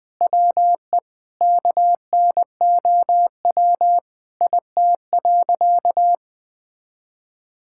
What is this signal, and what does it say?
Morse code, heard as a keyed tone, about 20 wpm, 700 Hz. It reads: WE KNOW IT.
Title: Sentences with 8-Word Phrases - 20wpm